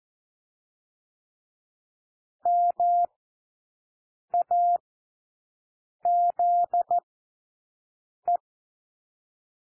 Morse code, heard as a keyed tone, about 14 wpm, spaced as for 6 wpm, 700 Hz. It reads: MAZE
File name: cwmp3 / W1 SOUND WORDS 14 wpm 0